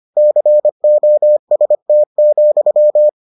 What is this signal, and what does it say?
Morse code, heard as a keyed tone, about 25 wpm, 600 Hz. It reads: COST,